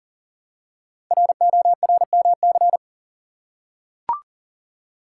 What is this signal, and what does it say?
Morse code, heard as a keyed tone, about 40 wpm, 700 Hz. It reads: RORMC
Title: Code Groups - 5 Letters - 40wpm